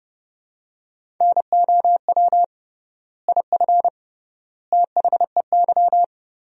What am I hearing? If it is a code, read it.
Morse code, heard as a keyed tone, about 30 wpm, 700 Hz. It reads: NOW IF THEY
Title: 3x Word-Spacing - Sentences from Top 500 Words - 30wpm